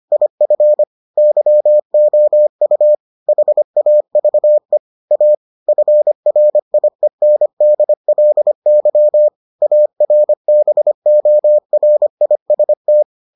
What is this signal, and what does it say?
Morse code, heard as a keyed tone, about 25 wpm, 600 Hz. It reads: IF YOU HAVE A FRIENDLY ARBORIST